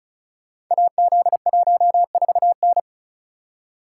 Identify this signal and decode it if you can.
Morse code, heard as a keyed tone, about 36 wpm, 690 Hz. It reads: AZ14N
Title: Code Groups - 5 Letter:Numbers - 35wpm